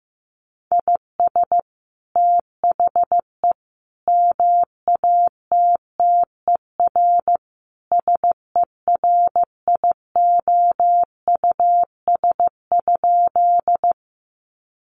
Morse code, heard as a keyed tone, about 15 wpm, 700 Hz. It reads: IS THE MATTER SERIOUS?
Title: Sentences from Top 1100 Words 15wpm